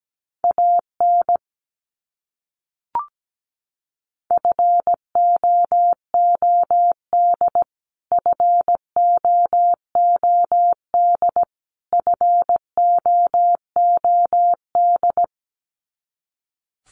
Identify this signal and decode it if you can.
Morse code, heard as a keyed tone, about 17 wpm, 700 Hz. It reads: AN FOOD FOOD FOOD
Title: Top 300 Words Repeated 17wpm